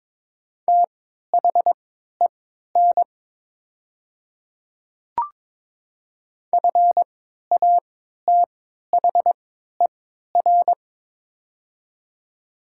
Morse code, heard as a keyed tone, about 22 wpm, spaced as for 13 wpm, 700 Hz. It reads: THEN FATHER
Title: 3x Character Spacing - Top 500 Words - 22wpm